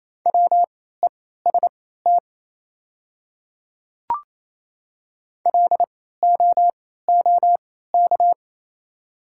Morse code, heard as a keyed tone, about 28 wpm, spaced as for 16 wpm, 690 Hz. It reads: WEST LOOK
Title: 3x Character Spacing - Top 500 Words - 28wpm